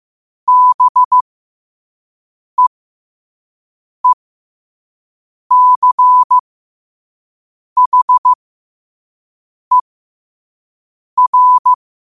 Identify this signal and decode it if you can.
Morse code, heard as a keyed tone, about 15 wpm, spaced as for 5 wpm, 1000 Hz. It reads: BEECHER
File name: qotdfiles / 5wpm